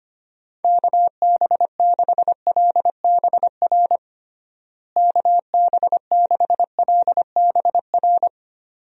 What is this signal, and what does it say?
Morse code, heard as a keyed tone, about 25 wpm, 700 Hz. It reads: KB6LBR KB6LBR
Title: USA Call Signs - 6 Characters - Repeated 2x - 25wpm